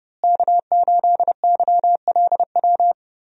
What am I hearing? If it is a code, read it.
Morse code, heard as a keyed tone, about 30 wpm, 700 Hz. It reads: K8YLW
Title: USA Call Signs - 5 Characters - Repeated 3x - 30wpm